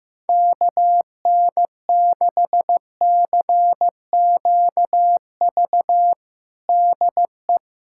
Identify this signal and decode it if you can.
Morse code, heard as a keyed tone, about 15 wpm, 700 Hz. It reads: KN6CQV DE
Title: USA Callsign DE Callsign - 6 Characters - 15wpm